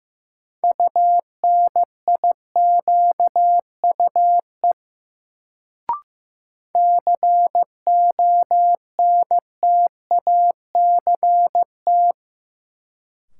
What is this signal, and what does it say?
Morse code, heard as a keyed tone, about 15 wpm, 700 Hz. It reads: UNIQUE CONTACT
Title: Sentences from Top 1000 Words - Review Part 7 - New Word Drill - 15wpm